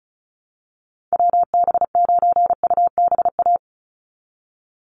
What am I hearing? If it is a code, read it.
Morse code, heard as a keyed tone, about 34 wpm, 700 Hz. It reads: WB9UBA
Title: USA Call Signs 35wpm